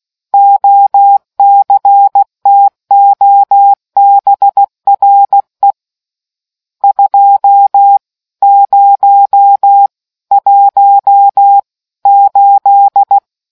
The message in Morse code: OCTOBRE 2018

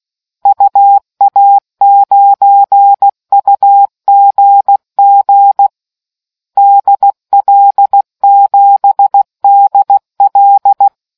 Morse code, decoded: UA9UGG DL7DL

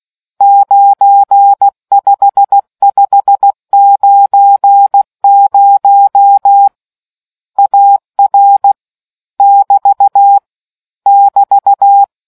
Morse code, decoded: 95590 AR <BT> <BT>

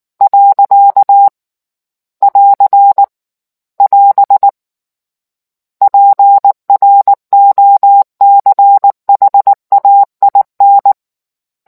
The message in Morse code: . <AR> <AS> PROCHAIN

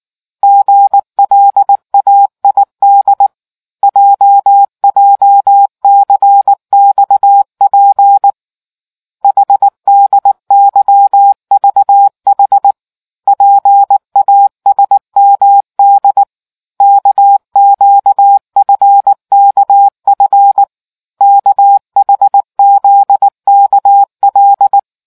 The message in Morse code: GLAID JJCXP HDYVH PASMD KQFKF KHZKL

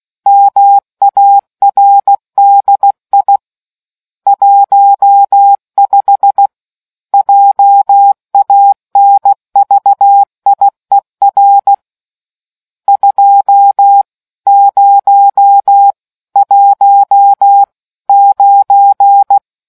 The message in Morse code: MARDI 15 JANVIER 2019

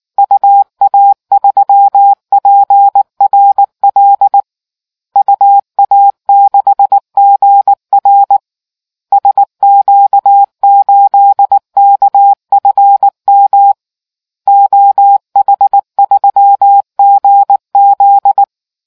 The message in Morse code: UA3PRL UA6GR SQ8KFM OH3GZ

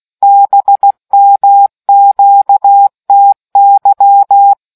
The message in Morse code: BMQTY

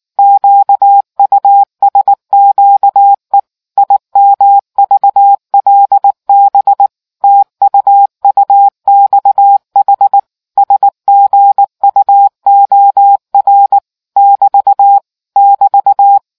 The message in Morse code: QUSQE IMVLB TUUXH SGUOR <BT> <BT>

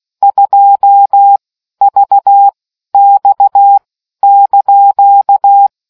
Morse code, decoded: 2VX )

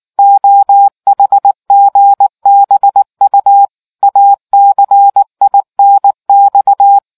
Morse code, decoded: OHGBU ACINX

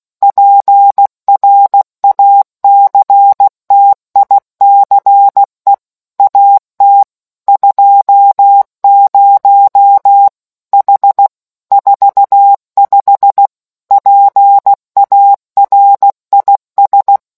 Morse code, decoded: PRACTICE AT 20 H 45 PARIS

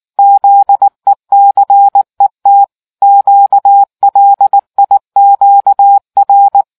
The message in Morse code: ZECET QLIQR

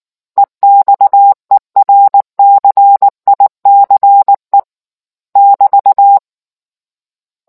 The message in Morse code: EXERCICE <BT>

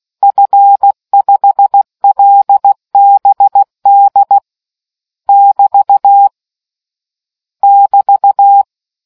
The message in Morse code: F5LBD <BT> <BT>

